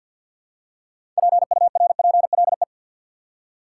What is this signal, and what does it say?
Morse code, heard as a keyed tone, about 52 wpm, 700 Hz. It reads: PURPLE
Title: Six Letter Words - 50wpm